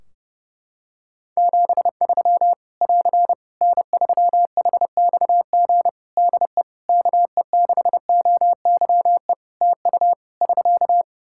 Morse code, heard as a keyed tone, about 32 wpm, 700 Hz. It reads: 73 <AR> N3HXG DE KE6OYE TU <SK>
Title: Practice QSO - Exchange 4 - Ending - 30wpm